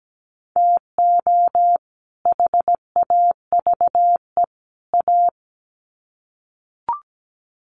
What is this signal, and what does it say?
Morse code, heard as a keyed tone, about 17 wpm, 700 Hz. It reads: TO HAVE A
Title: Sets of 3 Words 17wpm